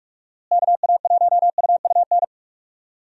Morse code, heard as a keyed tone, about 44 wpm, 700 Hz. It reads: KA1UUN
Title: USA Call Signs - 6 Characters - Repeated 2x - 45wpm